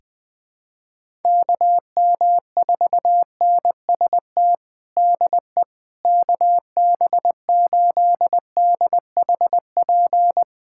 Morse code, heard as a keyed tone, about 20 wpm, 700 Hz. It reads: KM4NST DE KB8DHP
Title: USA Callsign DE Callsign 20wpm